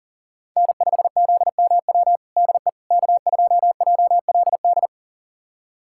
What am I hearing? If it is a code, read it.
Morse code, heard as a keyed tone, about 40 wpm, 700 Hz. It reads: N5ZMW DE K2JLD
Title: USA Callsign DE Callsign - 5 Characters - 40wpm